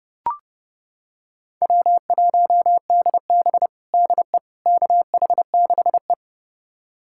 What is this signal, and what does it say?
Morse code, heard as a keyed tone, about 32 wpm, 700 Hz. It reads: W1DB DE KH6E
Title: USA Callsign DE Callsign - 4 Characters - 30wpm